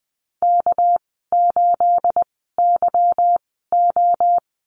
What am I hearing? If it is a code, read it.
Morse code, heard as a keyed tone, about 20 wpm, spaced as for 14 wpm, 700 Hz. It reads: K8YO
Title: 245 Lesson 39g - Letter z - Call Signs 20wpm